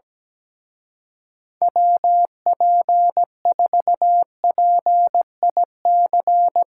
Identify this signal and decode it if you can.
Morse code, heard as a keyed tone, about 18 wpm, 700 Hz. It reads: WP4PIC
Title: Speed Racing - USA 2x3 Call Signs 17wpm